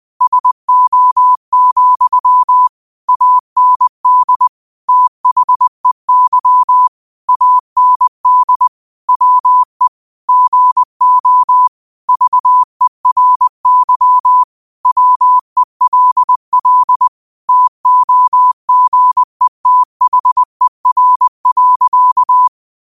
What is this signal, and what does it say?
Morse code, heard as a keyed tone, about 20 wpm, 1000 Hz. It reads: SO, AND THEY AND WE GO VERY WELL TOGETHER.
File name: 20wpm